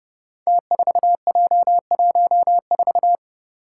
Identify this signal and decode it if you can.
Morse code, heard as a keyed tone, about 29 wpm, 700 Hz. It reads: T4J14
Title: Five Challenging Letter-Number Combinations 30wpm